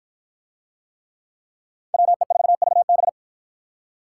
Morse code, heard as a keyed tone, about 49 wpm, 700 Hz. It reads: WE4VB